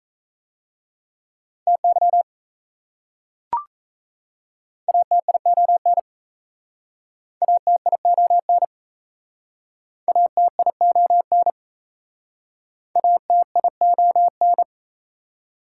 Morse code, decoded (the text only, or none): TY ATION ATION ATION ATION